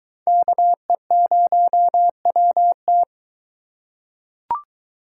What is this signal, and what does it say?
Morse code, heard as a keyed tone, about 22 wpm, 700 Hz. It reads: KE0WT